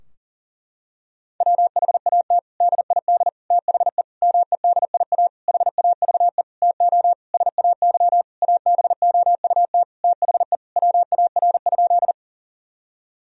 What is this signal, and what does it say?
Morse code, heard as a keyed tone, about 37 wpm, 700 Hz. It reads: WHAT DID THE MEDIA HAVE TO SAY ABOUT THE WAR?
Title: Sentences from Top 1000 Words - Review Part 4 - 40wpm